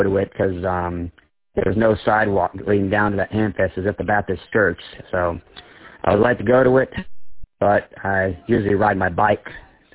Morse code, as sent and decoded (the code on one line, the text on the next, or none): none